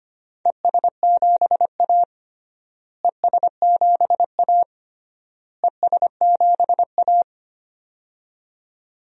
Morse code, heard as a keyed tone, about 25 wpm, 700 Hz. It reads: ES7A ES7A ES7A
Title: DXCC-100 Call Signs - 4 Characters - Repeated 3x - 25wpm